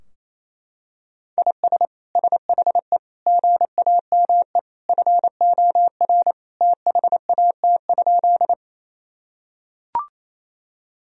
Morse code, heard as a keyed tone, about 28 wpm, 700 Hz. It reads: IS SHE GAME FOR THAT?
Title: Sentences from Top 1000 Words - Review Part 3 - 28wpm